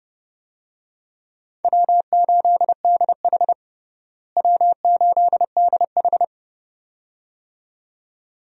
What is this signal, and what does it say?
Morse code, heard as a keyed tone, about 30 wpm, 700 Hz. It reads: W8DH W8DH
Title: USA Call Signs - 4 Characters - Repeated 2x - 30wpm